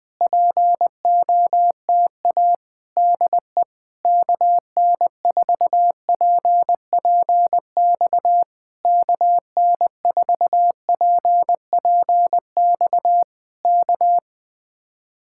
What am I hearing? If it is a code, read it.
Morse code, heard as a keyed tone, about 19 wpm, 700 Hz. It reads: POTA DE KN4PPX KN4PPX K
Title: Speed-Racing Common QSO Phrases - 20wpm